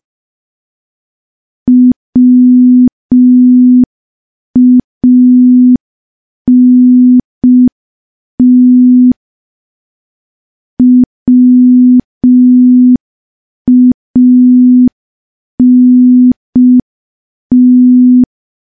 Morse code, heard as a keyed tone, about 5 wpm, 250 Hz. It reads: WANT WANT
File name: morse / morse-training-05WPM